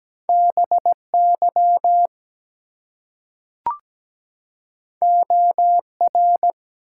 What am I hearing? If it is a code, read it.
Morse code, heard as a keyed tone, about 17 wpm, 700 Hz. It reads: BY OR
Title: Two Letter Words - 17wpm